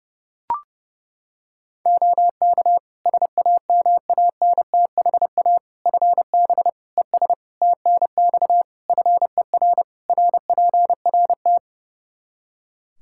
Morse code, heard as a keyed tone, about 32 wpm, 700 Hz. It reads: OK SAMANTHA FB ES TNX FER RPRT
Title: Practice QSO - Exchange 2 - Info - 30wpm